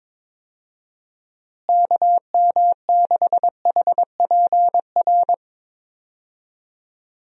Morse code, heard as a keyed tone, about 22 wpm, 700 Hz. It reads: KM6HPR
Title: USA Call Signs - 6 Characters - Repeated 3x - 22wpm